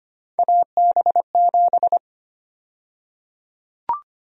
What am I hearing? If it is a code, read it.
Morse code, heard as a keyed tone, about 25 wpm, 700 Hz. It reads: AB7
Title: USA Call Sign Prefixes - Repeated 2x - 25wpm